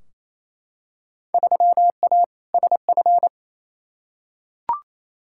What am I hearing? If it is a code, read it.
Morse code, heard as a keyed tone, about 28 wpm, 700 Hz. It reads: 3A SF
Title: Field Day Exchanges Encore - 28wpm